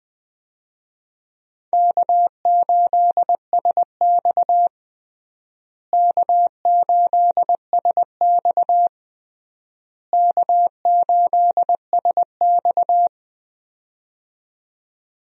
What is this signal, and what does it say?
Morse code, heard as a keyed tone, about 20 wpm, 700 Hz. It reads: K8SX K8SX K8SX
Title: USA Call Signs - 4 Characters - Repeated 3x - 20wpm